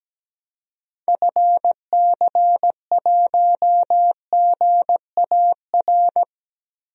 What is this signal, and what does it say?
Morse code, heard as a keyed tone, about 17 wpm, 700 Hz. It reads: FC1GAR
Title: DXCC-100 Call Signs - 6 Characters - Repeated 3x - 17wpm